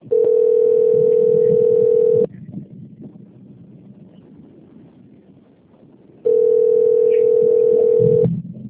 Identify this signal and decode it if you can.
Ringback tone (North America), 440 + 480 Hz.
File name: out- - - - - .84347